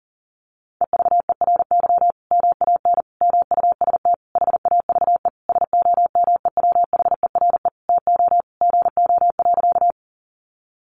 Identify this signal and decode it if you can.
Morse code, heard as a keyed tone, about 40 wpm, 700 Hz. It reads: EVERY MAN MUST HAVE SOMEWHERE TO GO.
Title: Sentences from Top 1400 Words 40wpm